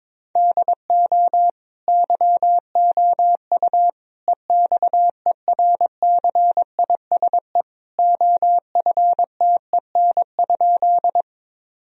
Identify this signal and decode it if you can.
Morse code, heard as a keyed tone, about 22 wpm, 700 Hz. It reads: DO YOU EXERCISE OFTEN?
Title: Sentences from Top 1000 Words - Review Part 4 - 22wpm